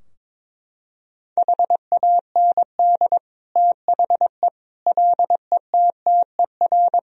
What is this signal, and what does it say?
Morse code, heard as a keyed tone, about 22 wpm, 700 Hz. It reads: HAND THE LETTER
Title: Sentences from Top 1000 Words - Review Part 2 - 22wpm